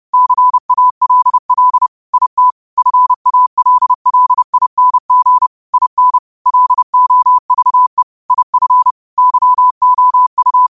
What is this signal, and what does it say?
Morse code, heard as a keyed tone, about 30 wpm, 1000 Hz. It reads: CALL IT FALLING IN LOVE IF YOU
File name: qotdfiles / 30wpm